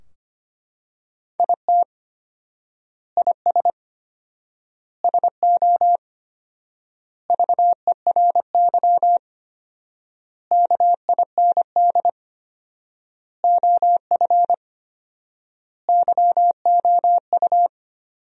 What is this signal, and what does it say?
Morse code, heard as a keyed tone, about 25 wpm, 700 Hz. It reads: IT IS SO VERY KIND OF YOU
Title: 4x Word-Spacing - Sentences from Top 300 Words - 25wpm